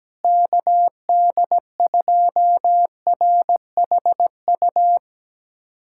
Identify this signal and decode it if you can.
Morse code, heard as a keyed tone, about 17 wpm, 700 Hz. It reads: KD2RHU
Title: USA Callsign DE Callsign - 6 Characters - 17wpm